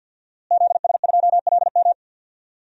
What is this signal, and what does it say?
Morse code, heard as a keyed tone, about 51 wpm, 700 Hz. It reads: ZS2LM
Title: DXCC-100 Call Signs - 5 Characters - Repeated 3x - 50wpm